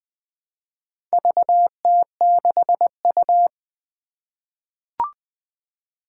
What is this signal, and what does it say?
Morse code, heard as a keyed tone, about 20 wpm, 700 Hz. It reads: VT6U
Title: Four Challenging Letter-Number Combinations 20wpm